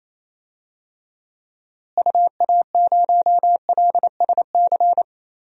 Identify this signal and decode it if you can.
Morse code, heard as a keyed tone, about 28 wpm, 700 Hz. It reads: UA0LSC
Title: DXCC-100 Call Signs - 6 Characters - Repeated 3x - 28wpm